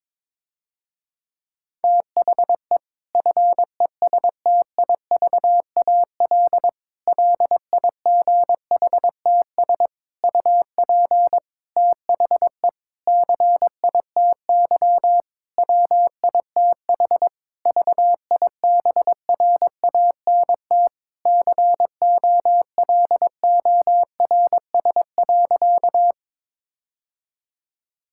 Morse code, heard as a keyed tone, about 21 wpm, 700 Hz. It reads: THE FESTIVAL LIGHTS UP THE CITY WITH VIBRANT COLORS.
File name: Difficult Sentences with Hints - 22wpm